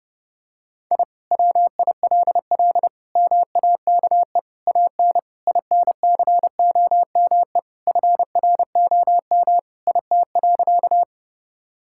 Morse code, heard as a keyed tone, about 30 wpm, 700 Hz. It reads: I WILL MAKE AN INCOME FROM IT.
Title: Sentences from Top 1000 Words - Review Part 5 - 30wpm